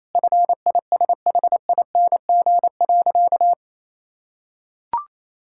Morse code, heard as a keyed tone, about 28 wpm, 700 Hz. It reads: FISHING.
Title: Sentences from Top 1000 Words - Review Part 6 - 28wpm